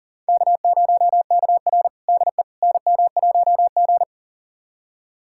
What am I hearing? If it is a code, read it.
Morse code, heard as a keyed tone, about 40 wpm, 700 Hz. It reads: K0KR DE NM1G
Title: USA Callsign DE Callsign - 4 Characters - 40wpm